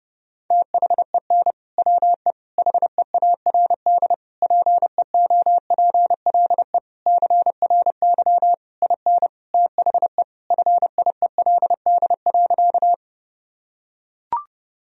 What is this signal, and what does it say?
Morse code, heard as a keyed tone, about 32 wpm, 700 Hz. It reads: THEN WE HEARD PEOPLE CRY IN THE FIELD.
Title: Sentences from Top 1000 Words - Review Part 2 - 30wpm